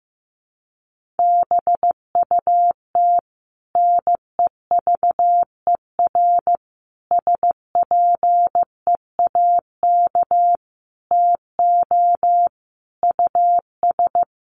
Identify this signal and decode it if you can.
Morse code, heard as a keyed tone, about 15 wpm, 700 Hz. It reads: BUT NEVER SPEAK TO US
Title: Sentences from Top 1100 Words 15wpm